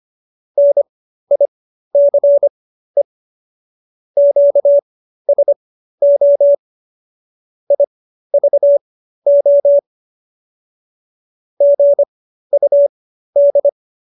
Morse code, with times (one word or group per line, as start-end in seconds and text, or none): 0.6-3.0 NICE
4.2-6.5 QSO
7.7-9.8 IVO
11.6-13.7 GUD